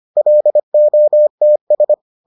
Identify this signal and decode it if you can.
Morse code, heard as a keyed tone, about 25 wpm, 600 Hz. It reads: LOTS